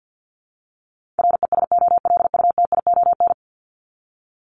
Morse code, heard as a keyed tone, about 50 wpm, 700 Hz. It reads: RESOLUTION
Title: Top 1501-1750 Words 50wpm